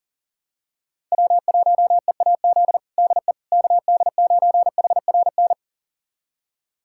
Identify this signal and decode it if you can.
Morse code, heard as a keyed tone, about 40 wpm, 700 Hz. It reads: W1EAZ DE KD9HRN
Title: USA Callsign DE Callsign 40wpm